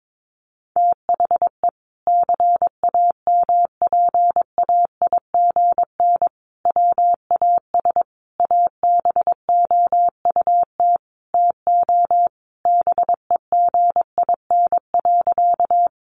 Morse code, heard as a keyed tone, about 22 wpm, 700 Hz. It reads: THE CAMPAIGN WAS ABOUT TO BEGIN.
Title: Sentences from Top 1200 Words 22wpm